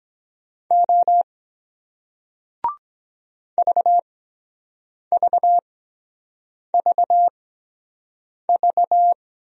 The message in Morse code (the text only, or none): O VVVV